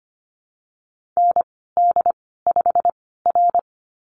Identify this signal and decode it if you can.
Morse code, heard as a keyed tone, about 25 wpm, spaced as for 16 wpm, 700 Hz. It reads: ND5R
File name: Lesson 15g - Number 5 - Call Signs 25wpm